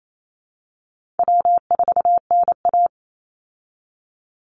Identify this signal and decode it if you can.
Morse code, heard as a keyed tone, about 28 wpm, 700 Hz. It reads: W4NA
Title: Call Signs Repeated 3x - 28wpm